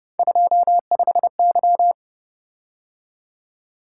Morse code, heard as a keyed tone, about 29 wpm, 700 Hz. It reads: 25Y